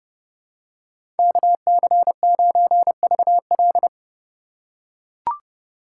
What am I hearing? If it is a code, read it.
Morse code, heard as a keyed tone, about 31 wpm, 700 Hz. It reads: KC9VL
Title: USA Call Signs - 5 Characters - 2x Character Spacing - 30wpm